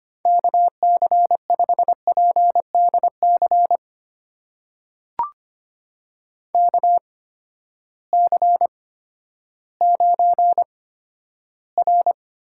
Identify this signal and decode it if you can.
Morse code, heard as a keyed tone, about 26 wpm, spaced as for 13 wpm, 700 Hz. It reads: KC5PDC KC9R
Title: USA Call Signs - 6 Characters - 8x Character Spacing - 25wpm